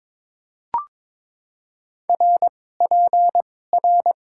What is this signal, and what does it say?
Morse code, heard as a keyed tone, about 22 wpm, spaced as for 16 wpm, 700 Hz. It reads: RPR
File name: Top 90 QSO Elements - 2x Character Spacing - 22wpm